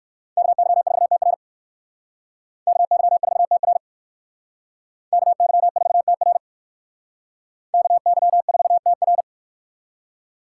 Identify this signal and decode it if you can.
Morse code, heard as a keyed tone, about 59 wpm, 700 Hz. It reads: KY4TR KY4TR KY4TR KY4TR